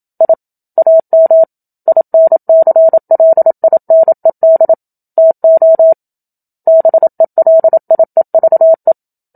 Morse code, heard as a keyed tone, about 27 wpm, 650 Hz. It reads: I AM INCLINED TO BELIEVE
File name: Book Two / 01-Chapter 1 - 27 WPM